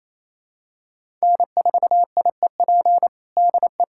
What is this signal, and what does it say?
Morse code, heard as a keyed tone, about 27 wpm, 700 Hz. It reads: N4IEP DE
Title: USA Callsign DE Callsign 28wpm